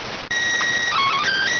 SSTV (slow-scan television), PD 290.